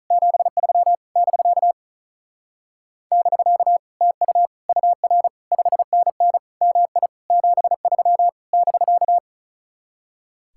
Morse code, heard as a keyed tone, about 46 wpm, 700 Hz. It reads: 73 <BK> <BK> TU UR 5NN MI 73 <BK>